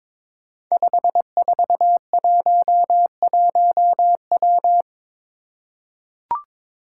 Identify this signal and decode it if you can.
Morse code, heard as a keyed tone, about 22 wpm, 700 Hz. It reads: 5411W
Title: Five Challenging Letter-Number Combinations 22wpm